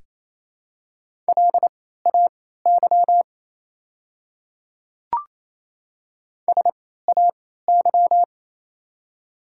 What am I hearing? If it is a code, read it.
Morse code, heard as a keyed tone, about 27 wpm, spaced as for 16 wpm, 700 Hz. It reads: LAY SAY
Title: 3x Character Spacing - Top 500 Words - 28wpm